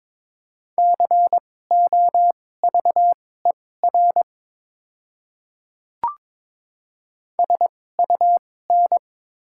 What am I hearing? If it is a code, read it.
Morse code, heard as a keyed tone, about 22 wpm, spaced as for 16 wpm, 700 Hz. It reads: COVER SUN
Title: 2x Character Spacing - Top 300 Words - 22wpm